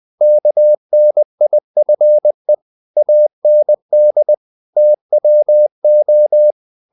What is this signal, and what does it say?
Morse code, heard as a keyed tone, about 20 wpm, 600 Hz. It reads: KNIFE AND TWO
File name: the old man and the sea part 2 20wpm